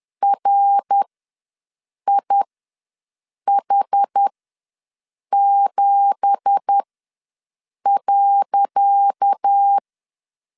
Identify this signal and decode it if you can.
Morse code, heard as a keyed tone, about 10 wpm, spaced as for 6 wpm, 800 Hz. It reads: RIH7.
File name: LEZIONI / 10 SIGNS / 10C DOT ws 10wpm 4213